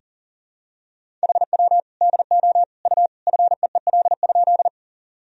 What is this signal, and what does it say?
Morse code, heard as a keyed tone, about 40 wpm, 700 Hz. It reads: HW DO U FEEL?
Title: Sentences from Top 400 Words and Abbreviations 40wpm